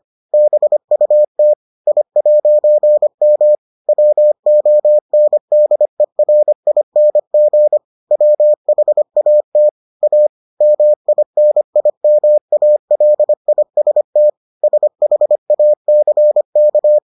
Morse code, heard as a keyed tone, about 25 wpm, 600 Hz. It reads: BUT I'M WONDERING WHAT A MINIMALIST SHACK